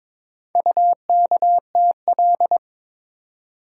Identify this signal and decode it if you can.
Morse code, heard as a keyed tone, about 22 wpm, 700 Hz. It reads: UKTL
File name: practice / 22wpm / Four Letter-Number Combinations - 22wpm